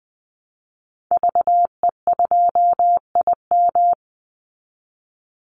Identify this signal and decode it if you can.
Morse code, heard as a keyed tone, about 20 wpm, 700 Hz. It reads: VE2IM
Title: Call Signs Repeated 2x - 20wpm